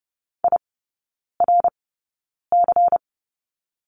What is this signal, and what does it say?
Morse code, heard as a keyed tone, about 30 wpm, spaced as for 9 wpm, 700 Hz. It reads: IRC
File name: Lesson 16e - Letter c - Three Characters 30wpm